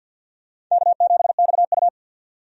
Morse code, heard as a keyed tone, about 53 wpm, 700 Hz. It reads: K7XU
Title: USA Call Signs - 4 Characters - 50wpm